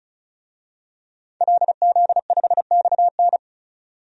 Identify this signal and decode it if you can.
Morse code, heard as a keyed tone, about 34 wpm, 700 Hz. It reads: LZ5XN
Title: DXCC-100 Call Signs - 5 Characters - 35wpm